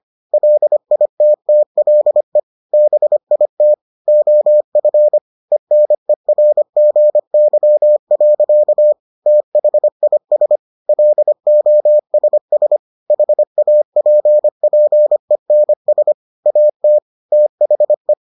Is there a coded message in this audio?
Yes: Morse code from 0.3 to 18.1 s, heard as a keyed tone, about 25 wpm, 600 Hz. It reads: LITTLE BIT OF ENERGY. THIS LOSS HAPPENS AT THE